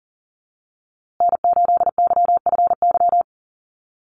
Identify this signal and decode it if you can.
Morse code, heard as a keyed tone, about 40 wpm, 700 Hz. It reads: N8YFY
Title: USA Call Signs Repeated 2x 40wpm